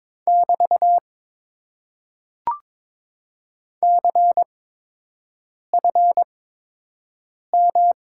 Morse code, heard as a keyed tone, about 22 wpm, spaced as for 6 wpm, 700 Hz. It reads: <BT> CFM
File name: Top 90 QSO Elements - 8x Character Spacing - 22wpm